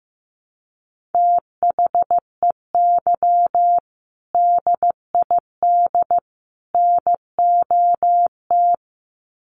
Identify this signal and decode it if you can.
Morse code, heard as a keyed tone, about 15 wpm, 700 Hz. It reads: THEY DID NOT
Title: Sets of 3 Words 15wpm